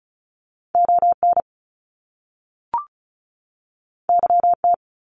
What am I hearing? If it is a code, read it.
Morse code, heard as a keyed tone, about 36 wpm, 700 Hz. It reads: ON YT
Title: Canadian Provinces and Territories 35wpm